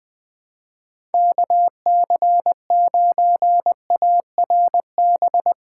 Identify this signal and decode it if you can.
Morse code, heard as a keyed tone, about 21 wpm, 700 Hz. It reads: KC9ARB